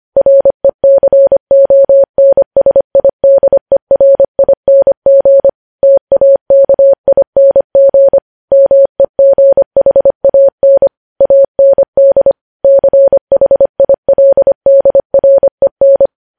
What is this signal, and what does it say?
Morse code, heard as a keyed tone, about 25 wpm, 550 Hz. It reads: RECONSIDERING TAKING MEGHAN AND CHILDREN
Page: news-headlines-in-morse-code-at-25-wpm